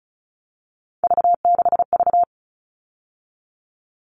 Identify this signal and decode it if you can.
Morse code, heard as a keyed tone, about 34 wpm, 700 Hz. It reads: V6V
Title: All Cut Numbers - 3-Digit 35wpm